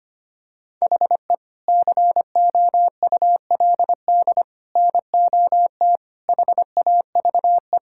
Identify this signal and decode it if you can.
Morse code, heard as a keyed tone, about 25 wpm, 700 Hz. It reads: HE COULD NOT HAVE
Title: Sentences from Top 1600 Words 25wpm